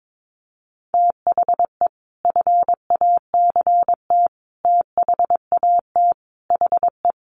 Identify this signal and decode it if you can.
Morse code, heard as a keyed tone, about 22 wpm, 700 Hz. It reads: THE FACT THAT HE
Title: Sets of 5 Words 22wpm